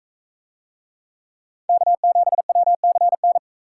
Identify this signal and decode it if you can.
Morse code, heard as a keyed tone, about 44 wpm, 700 Hz. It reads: K7WCN